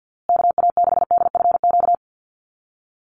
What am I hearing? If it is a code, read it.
Morse code, heard as a keyed tone, about 50 wpm, 700 Hz. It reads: KA6DFQ
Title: USA Call Signs Repeated 2x 50wpm